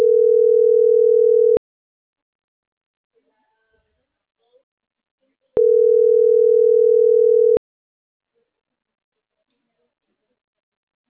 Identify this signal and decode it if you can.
Ringback tone (North America), 440 + 480 Hz.